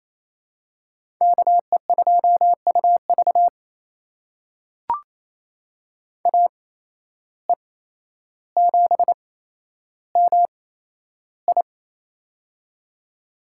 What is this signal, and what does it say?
Morse code, heard as a keyed tone, about 28 wpm, spaced as for 12 wpm, 700 Hz. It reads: KE2UV AE7MI
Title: USA Call Signs - 5 Characters - 8x Character Spacing - 28wpm